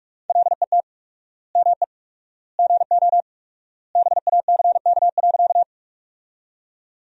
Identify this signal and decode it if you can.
Morse code, heard as a keyed tone, about 47 wpm, 700 Hz. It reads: LET ME GO BACK.
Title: 4x Word-Spacing - Sentences from Top 300 Words - 45wpm